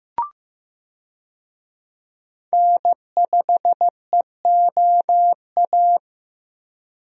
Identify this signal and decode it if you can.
Morse code, heard as a keyed tone, about 15 wpm, 700 Hz. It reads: N5EOA
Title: USA Call Signs - 5 Characters - Repeated 2x - 15wpm